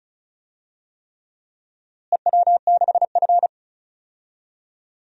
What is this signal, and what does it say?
Morse code, heard as a keyed tone, about 34 wpm, 700 Hz. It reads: EW6F
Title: DXCC-100 Call Signs - 4 Characters - Repeated 3x - 35wpm